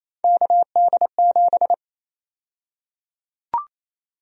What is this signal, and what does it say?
Morse code, heard as a keyed tone, about 28 wpm, 700 Hz. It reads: KD7
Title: USA Call Sign Prefixes - Repeated 2x - 28wpm